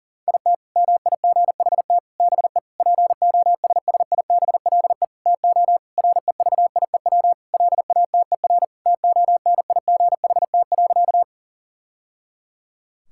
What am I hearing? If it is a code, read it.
Morse code, heard as a keyed tone, about 40 wpm, 700 Hz. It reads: IT MIGHT BE POSSIBLE TO REVIEW LATER TONIGHT.
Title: Sentences from Top 1000 Words - Review Part 6 - 40wpm